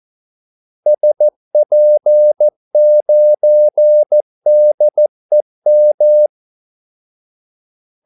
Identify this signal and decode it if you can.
Morse code, heard as a keyed tone, about 14 wpm, 600 Hz. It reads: SP9DEM
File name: Callsigns-14wpm